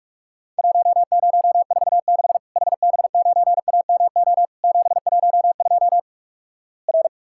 Morse code, heard as a keyed tone, about 48 wpm, 700 Hz. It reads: 1046 HB9AMO 712 R
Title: Worked All Europe DX Contest - QTC Exchange - 45wpm